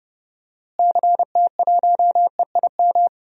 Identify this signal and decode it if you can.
Morse code, heard as a keyed tone, about 31 wpm, 700 Hz. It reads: CT1EIM
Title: DXCC-100 Call Signs - 6 Characters - Repeated 3x - 30wpm